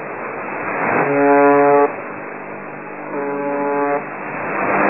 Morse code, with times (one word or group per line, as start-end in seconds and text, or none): none